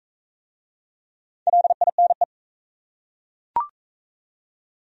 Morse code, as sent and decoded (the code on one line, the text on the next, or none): .-.. .. -. .
LINE